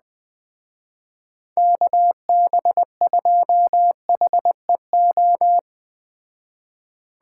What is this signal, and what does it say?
Morse code, heard as a keyed tone, about 20 wpm, 700 Hz. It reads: KB2HEO
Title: Speed Racing - USA 2x3 Call Signs 17wpm